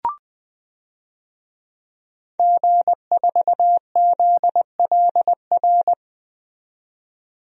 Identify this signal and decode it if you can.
Morse code, heard as a keyed tone, about 20 wpm, 700 Hz. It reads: G4ZLR
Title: DXCC-100 Call Signs - 5 Characters - 20wpm